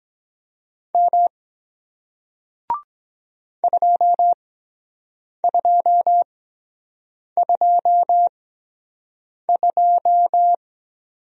Morse code, heard as a keyed tone, about 23 wpm, spaced as for 7 wpm, 700 Hz. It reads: M 2222